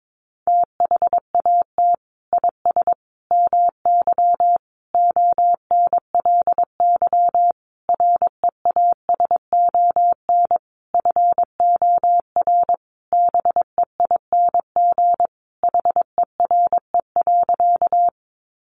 Morse code, heard as a keyed tone, about 22 wpm, 700 Hz. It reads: THAT IS MY ONLY REASON FOR BEING HERE.
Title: Sentences from Top 700 Words 22wpm